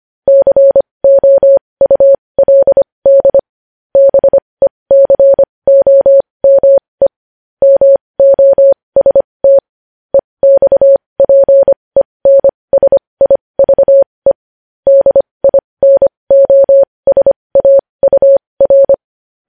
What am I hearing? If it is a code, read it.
Morse code, heard as a keyed tone, about 25 wpm, spaced as for 20 wpm, 550 Hz. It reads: COULD BECOME MOST EXPENSIVE DINOSAUR
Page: news headlines in morse code at 20 wpm